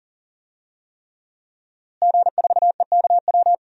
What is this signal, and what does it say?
Morse code, heard as a keyed tone, about 37 wpm, 700 Hz. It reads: G4EKW
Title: DXCC-100 Call Signs - 5 Characters - Repeated 3x - 40wpm